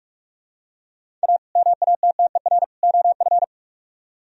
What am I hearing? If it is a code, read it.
Morse code, heard as a keyed tone, about 46 wpm, 700 Hz. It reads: A MATTER OF